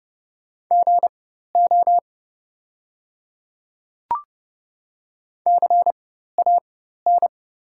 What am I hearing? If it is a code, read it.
Morse code, heard as a keyed tone, about 30 wpm, spaced as for 14 wpm, 700 Hz. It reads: GO CAN
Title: 4x Character Spacing - Top 100 Words - 30wpm